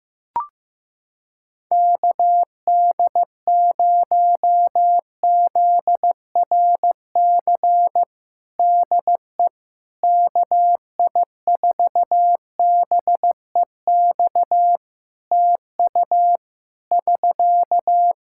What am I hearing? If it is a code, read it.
Morse code, heard as a keyed tone, about 15 wpm, 700 Hz. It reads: KD0ZRC DE KI4BEX TU <SK>
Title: Practice QSO - Exchange 4 - Ending - 15wpm